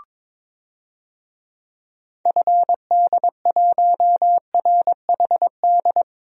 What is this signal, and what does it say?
Morse code, heard as a keyed tone, about 22 wpm, 700 Hz. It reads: FD1RHD
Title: DXCC-100 Call Signs - 6 Characters - Repeated 2x - 22wpm